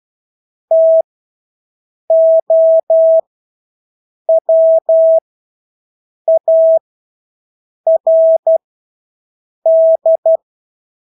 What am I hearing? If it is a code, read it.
Morse code, heard as a keyed tone, about 12 wpm, spaced as for 6 wpm, 650 Hz. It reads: TOWARD
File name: Engelske ord 12-6 wpm